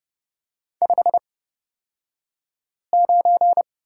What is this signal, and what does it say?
Morse code, heard as a keyed tone, about 30 wpm, spaced as for 5 wpm, 700 Hz. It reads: 59
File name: Lesson 38d - Letter q - Two Characters 30wpm